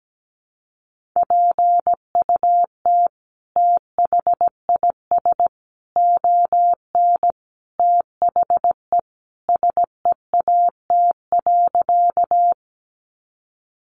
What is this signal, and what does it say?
Morse code, heard as a keyed tone, about 17 wpm, 700 Hz. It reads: PUT THIS ON THE SEAT.
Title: Sentences from Top 1500 Words 17wpm